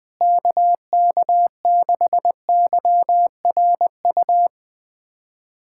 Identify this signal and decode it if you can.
Morse code, heard as a keyed tone, about 19 wpm, 700 Hz. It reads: KK6YRU
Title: USA Callsign DE Callsign - 6 Characters - 20wpm